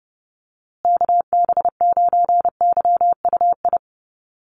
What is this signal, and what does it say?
Morse code, heard as a keyed tone, about 28 wpm, 700 Hz. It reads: KB9YUI
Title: USA Call Signs Repeated 3x 30wpm